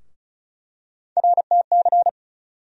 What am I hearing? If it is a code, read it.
Morse code, heard as a keyed tone, about 36 wpm, 700 Hz. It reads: RTC
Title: USA Call Sign Postfixes - 3 Letters 35wpm